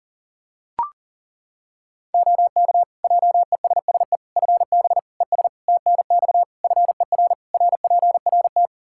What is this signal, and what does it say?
Morse code, heard as a keyed tone, about 40 wpm, 700 Hz. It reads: OK JESSE FB ES TNX FER RPRT
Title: Practice QSO - Exchange 2 - Info - 40wpm